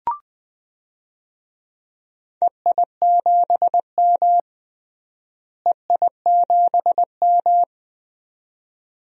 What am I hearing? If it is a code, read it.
Morse code, heard as a keyed tone, about 20 wpm, 700 Hz. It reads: EI7M EI7M
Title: DXCC-100 Call Signs - 4 Characters - Repeated 2x - 20wpm